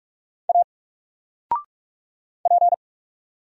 Morse code, heard as a keyed tone, about 44 wpm, 700 Hz. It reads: A P
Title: Single Letter-Number Alt 45wpm